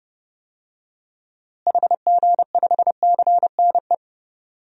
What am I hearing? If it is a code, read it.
Morse code, heard as a keyed tone, about 30 wpm, 700 Hz. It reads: HG5CNE
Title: DXCC-100 Call Signs - 6 Characters - Repeated 2x - 30wpm